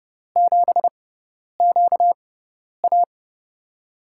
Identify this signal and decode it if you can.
Morse code, heard as a keyed tone, about 32 wpm, spaced as for 10 wpm, 700 Hz. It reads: 7QA
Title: USA Call Signs - 4 Characters - 6x Character Spacing - 30wpm